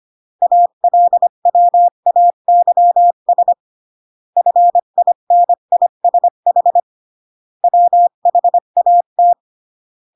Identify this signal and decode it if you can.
Morse code, heard as a keyed tone, about 25 wpm, 700 Hz. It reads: ALWAYS FINISH WHAT
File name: CW / jodorowsky / 82 commandments 25-20